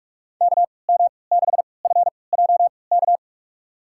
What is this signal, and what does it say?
Morse code, heard as a keyed tone, about 46 wpm, spaced as for 26 wpm, 700 Hz. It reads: KM6FJK
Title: USA Call Signs - 6 Characters - 3x Character Spacing - 45wpm